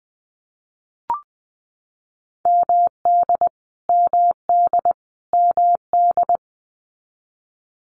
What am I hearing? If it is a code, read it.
Morse code, heard as a keyed tone, about 20 wpm, 700 Hz. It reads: MD MD MD
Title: US States Repeated 3x 20wpm